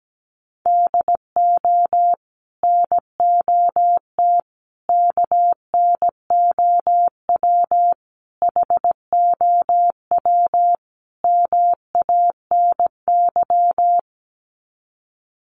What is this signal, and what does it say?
Morse code, heard as a keyed tone, about 17 wpm, 700 Hz. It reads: DO NOT KNOW HOW MANY
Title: Sets of 5 Words Encore 17wpm